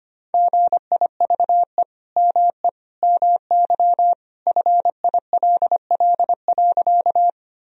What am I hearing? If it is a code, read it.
Morse code, heard as a keyed tone, about 25 wpm, 700 Hz. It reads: GIVE ME MY FILL.
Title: Sentences from Top 1000 Words - Review Part 3 - 25wpm